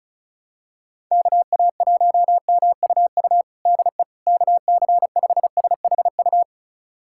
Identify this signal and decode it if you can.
Morse code, heard as a keyed tone, about 36 wpm, 700 Hz. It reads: KA1MUU DE KC5SHU
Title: USA Callsign DE Callsign - 6 Characters - 35wpm